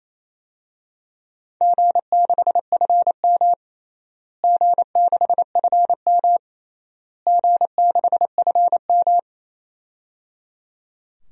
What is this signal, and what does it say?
Morse code, heard as a keyed tone, about 27 wpm, 700 Hz. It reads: G6FM G6FM G6FM
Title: DXCC-100 Call Signs - 4 Characters - Repeated 3x - 28wpm